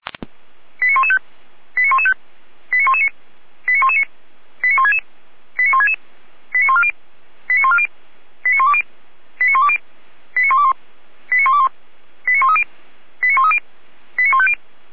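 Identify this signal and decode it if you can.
Five-tone selective call (ZVEI), heard as quick runs of five tones, tones of about 70 ms. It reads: 89115 89115 89119 89119 89166 89166 89140 89140 89120 89120 89121 89121 89130 89130 89150
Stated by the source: ZVEI five-tone selective call received by SDR, from the Feuerwehr Orlamünde (fire brigade) audio archive.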